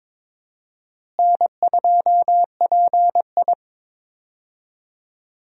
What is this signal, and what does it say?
Morse code, heard as a keyed tone, about 22 wpm, 700 Hz. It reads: N2PI
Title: USA Call Signs - 4 Characters - Repeated 3x - 22wpm